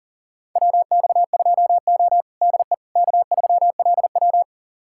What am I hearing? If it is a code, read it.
Morse code, heard as a keyed tone, about 40 wpm, 700 Hz. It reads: WX2O DE K3LW
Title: USA Callsign DE Callsign - 4 Characters - 40wpm